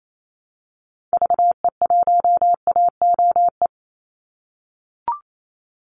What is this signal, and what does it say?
Morse code, heard as a keyed tone, about 28 wpm, 700 Hz. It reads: VE1AOE
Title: Call Signs - 28wpm